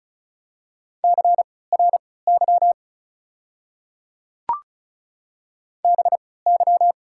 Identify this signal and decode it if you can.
Morse code, heard as a keyed tone, about 34 wpm, spaced as for 20 wpm, 690 Hz. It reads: CRY BY
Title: 3x Character Spacing - Top 500 Words - 35wpm